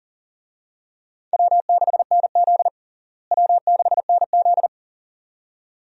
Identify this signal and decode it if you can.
Morse code, heard as a keyed tone, about 40 wpm, 700 Hz. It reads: W6NZ W6NZ
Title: USA Call Signs - 4 Characters - Repeated 2x - 40wpm